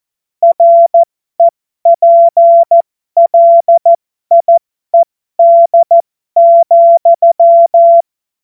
Morse code, heard as a keyed tone, about 14 wpm, spaced as for 12 wpm, 680 Hz. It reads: REPLIED,
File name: Morsekurs / Witze